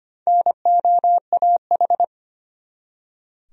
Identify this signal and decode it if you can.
Morse code, heard as a keyed tone, about 25 wpm, 700 Hz. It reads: NOAH
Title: USA Common Names - Max 4 Letters - 25wpm